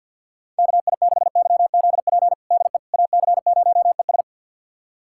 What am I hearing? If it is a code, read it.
Morse code, heard as a keyed tone, about 46 wpm, 700 Hz. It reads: KI6YZP DE AC0ES
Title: USA Callsign DE Callsign 50wpm